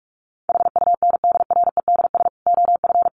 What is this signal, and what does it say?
Morse code, heard as a keyed tone, about 44 wpm, 700 Hz. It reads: HUNDREDS OF
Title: Sets of 5 Words 45wpm